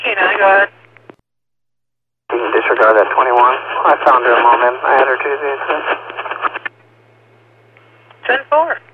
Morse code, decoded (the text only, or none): none